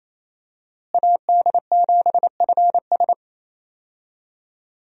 Morse code, heard as a keyed tone, about 28 wpm, 700 Hz. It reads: AD7FS
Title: USA Call Signs - 5 Characters - Repeated 2x - 28wpm